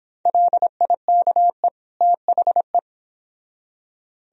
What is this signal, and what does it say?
Morse code, heard as a keyed tone, about 30 wpm, 700 Hz. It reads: LIKE THE